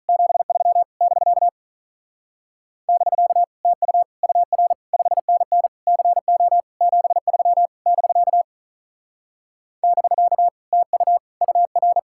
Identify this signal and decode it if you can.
Morse code, heard as a keyed tone, about 45 wpm, 700 Hz. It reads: 73 <BK> <BK> TU UR 5NN CO 73 <BK> <BK> TU UR